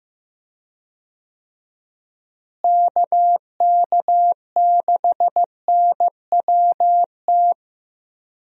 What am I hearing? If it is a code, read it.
Morse code, heard as a keyed tone, about 15 wpm, 700 Hz. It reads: KK6NWT
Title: USA Call Signs - 6 Characters - Repeated 2x - 15wpm